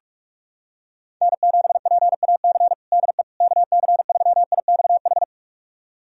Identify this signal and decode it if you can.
Morse code, heard as a keyed tone, about 46 wpm, 700 Hz. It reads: N7PAC DE KC3IXH
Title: USA Callsign DE Callsign 45wpm